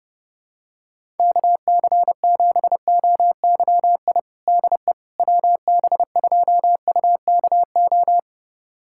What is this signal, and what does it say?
Morse code, heard as a keyed tone, about 29 wpm, 700 Hz. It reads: KC7OYI DE WB2UKO
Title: USA Callsign DE Callsign - 6 Characters - 30wpm